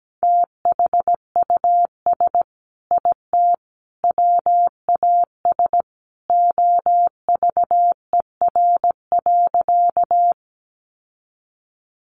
Morse code, heard as a keyed tone, about 17 wpm, 700 Hz. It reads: THUS IT WAS OVER.
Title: Sentences from Top 800 Words 17wpm